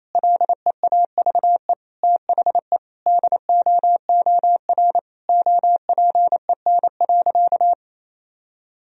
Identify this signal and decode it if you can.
Morse code, heard as a keyed tone, about 28 wpm, 700 Hz. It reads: LEAVE THE DOOR OPEN.
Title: Sentences from Top 1000 Words - Review Part 1 - 28wpm